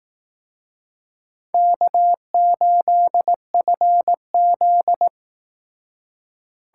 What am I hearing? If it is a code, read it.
Morse code, heard as a keyed tone, about 19 wpm, 700 Hz. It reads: K8FZ